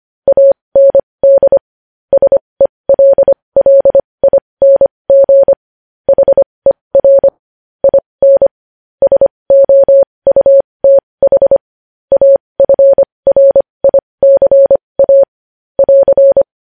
Morse code, heard as a keyed tone, about 25 wpm, spaced as for 20 wpm, 550 Hz. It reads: AND SELLING HER IN SOUTH AFRICA <AR>
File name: podcast / News headlines-1x-25x20